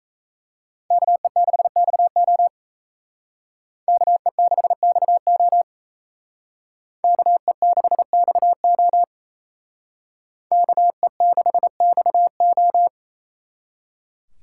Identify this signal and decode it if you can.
Morse code, heard as a keyed tone, about 36 wpm, 700 Hz. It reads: KE6XO KE6XO KE6XO KE6XO